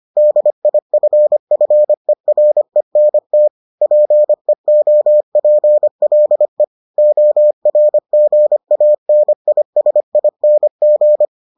Morse code, heard as a keyed tone, about 25 wpm, 600 Hz. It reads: DIFFERENT PEOPLE ORGANISING